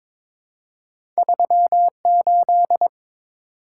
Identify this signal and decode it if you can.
Morse code, heard as a keyed tone, about 22 wpm, 700 Hz. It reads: 38
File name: practice / 22wpm / Two Letter-Number Combinations - 22wpm